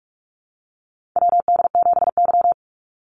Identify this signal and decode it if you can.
Morse code, heard as a keyed tone, about 46 wpm, 690 Hz. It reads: WD7Y
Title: NA SOTA Chaser Callsigns 45wpm